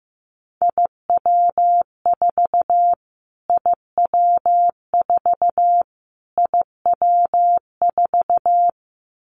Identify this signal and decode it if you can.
Morse code, heard as a keyed tone, about 15 wpm, 700 Hz. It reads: IW4 IW4 IW4
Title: Callsign Prefixes for Top 25 DXCC - Repeated 3x - 15wpm